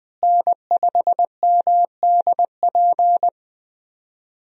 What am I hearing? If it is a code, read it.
Morse code, heard as a keyed tone, about 20 wpm, 700 Hz. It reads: N5MDP
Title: USA Call Signs - 5 Characters - Repeated 3x - 20wpm